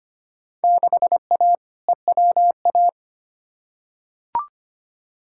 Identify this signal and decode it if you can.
Morse code, heard as a keyed tone, about 25 wpm, 700 Hz. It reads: 6A EWA
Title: Field Day Exchanges Encore - 25wpm